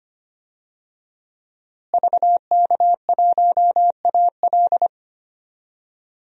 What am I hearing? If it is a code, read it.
Morse code, heard as a keyed tone, about 25 wpm, 700 Hz. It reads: VK1AL
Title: DXCC-100 Call Signs - 5 Characters - Repeated 3x - 25wpm